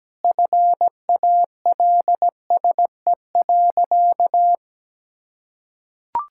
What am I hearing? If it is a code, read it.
Morse code, heard as a keyed tone, about 17 wpm, 700 Hz. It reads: FALSE.
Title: Sentences from Top 1600 Words 17wpm